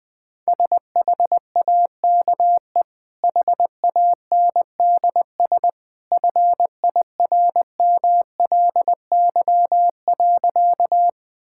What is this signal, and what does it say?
Morse code, heard as a keyed tone, about 20 wpm, 700 Hz. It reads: SHAKE HANDS FIRMLY.
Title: Easy Sentences with Hints - 20wpm